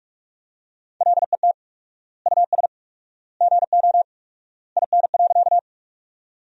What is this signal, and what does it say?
Morse code, heard as a keyed tone, about 41 wpm, 700 Hz. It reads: LET US GO IN.
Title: 4x Word-Spacing - Sentences from Top 300 Words - 45wpm